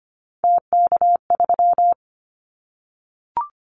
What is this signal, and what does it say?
Morse code, heard as a keyed tone, about 25 wpm, 700 Hz. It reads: TK3
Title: Callsign Prefixes for Top 100 DXCC - 25wpm